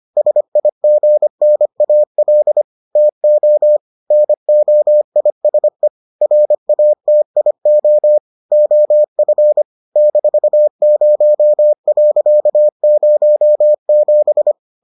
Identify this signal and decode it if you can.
Morse code, heard as a keyed tone, about 25 wpm, 600 Hz. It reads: SIGNAL TO NOISE RATIO OF -0.07